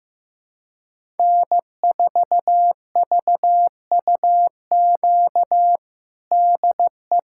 Morse code, heard as a keyed tone, about 16 wpm, 700 Hz. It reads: N4VUQ DE
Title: USA Callsign DE Callsign 15wpm